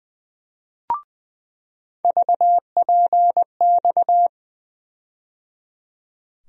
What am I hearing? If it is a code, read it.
Morse code, heard as a keyed tone, about 20 wpm, 700 Hz. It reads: VPX
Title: USA Call Sign Postfixes - 3 Letters 20wpm